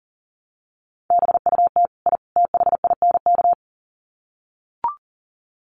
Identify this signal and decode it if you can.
Morse code, heard as a keyed tone, about 40 wpm, 700 Hz. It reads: BUT I THINK
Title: Sets of 3 Words 40wpm